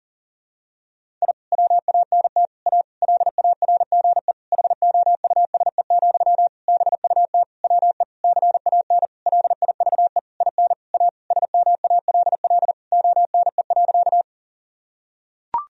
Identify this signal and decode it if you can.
Morse code, heard as a keyed tone, about 40 wpm, 700 Hz. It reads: I WANT A LARGE HOUSE, BUT WE CAN LIVE IN A SMALL ONE.
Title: Sentences from Top 1000 Words - Review Part 1 - 40wpm